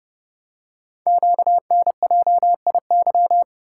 Morse code, heard as a keyed tone, about 30 wpm, 690 Hz. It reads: QNJIY
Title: Code Groups - 5 Letters - 30wpm